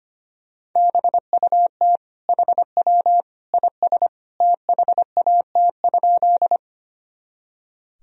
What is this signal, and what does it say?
Morse code, heard as a keyed tone, about 25 wpm, 700 Hz. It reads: BUT HW IS THAT?
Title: Sentences from Top 100 Words and Abbreviations 25wpm